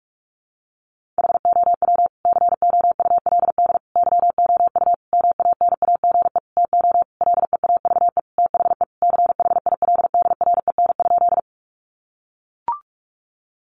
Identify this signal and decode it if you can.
Morse code, heard as a keyed tone, about 44 wpm, 700 Hz. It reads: HOW COULD YOU MANAGE TO LEAVE THE CHILDREN?
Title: Sentences from Top 1400 Words 45wpm